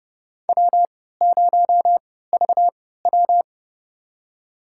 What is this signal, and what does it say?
Morse code, heard as a keyed tone, about 30 wpm, spaced as for 17 wpm, 700 Hz. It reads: W0VW
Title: USA Call Signs - 4 Characters - 3x Character Spacing - 30wpm